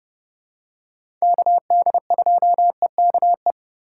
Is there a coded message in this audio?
Yes: Morse code from 1.2 to 3.5 s, heard as a keyed tone, about 30 wpm, 700 Hz. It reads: KD2EKE